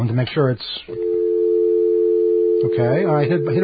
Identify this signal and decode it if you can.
Dial tone (North America), 350 + 440 Hz, with speech.
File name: mp3files / off the hook overtime